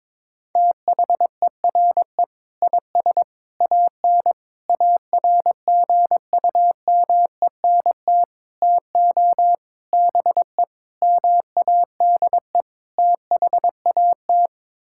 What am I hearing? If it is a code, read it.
Morse code, heard as a keyed tone, about 22 wpm, 700 Hz. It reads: THERE IS AN ARGUMENT TO BE MADE THAT
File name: practice / 22wpm / Sets of 8 Words - 22wpm